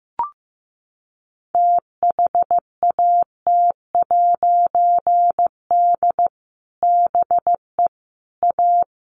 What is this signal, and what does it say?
Morse code, heard as a keyed tone, about 15 wpm, 700 Hz. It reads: THAT'D BE A
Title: A Sound Of Thunder 15wpm